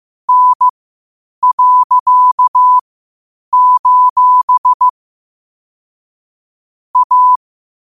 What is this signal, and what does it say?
Morse code, heard as a keyed tone, about 15 wpm, spaced as for 8 wpm, 1000 Hz. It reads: N.: A
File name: qotdfiles / 8wpm